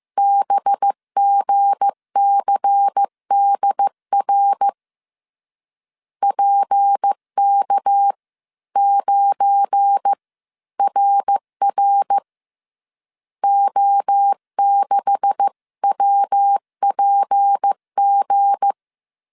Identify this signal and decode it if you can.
Morse code, heard as a keyed tone, about 15 wpm, spaced as for 13 wpm, 800 Hz. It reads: BGCDR PK9RR O6WPG
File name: CW / DET / 1 9 1500C S A 14wpm 3222